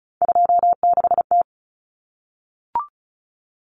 Morse code, heard as a keyed tone, about 34 wpm, 700 Hz. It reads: 26T
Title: Common Cut Numbers - 3-Digit 35wpm